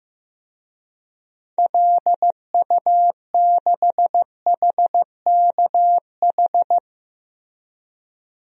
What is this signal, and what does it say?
Morse code, heard as a keyed tone, about 15 wpm, 700 Hz. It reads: LU6HKH
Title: DXCC-100 Call Signs - 6 Characters - Repeated 3x - 15wpm